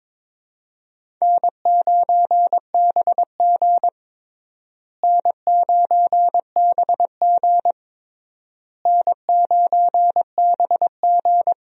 Morse code, heard as a keyed tone, about 22 wpm, 700 Hz. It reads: N9BG N9BG N9BG
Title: USA Call Signs - 4 Characters - Repeated 3x - 22wpm